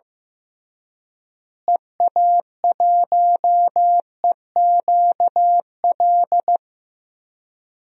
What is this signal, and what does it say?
Morse code, heard as a keyed tone, about 16 wpm, 700 Hz. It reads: EA1EQL
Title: DXCC-100 Call Signs - 6 Characters - Repeated 2x - 15wpm